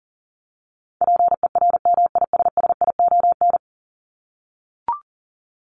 Morse code, heard as a keyed tone, about 40 wpm, 700 Hz. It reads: PERMISSION
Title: Top 1501-1750 Words 40wpm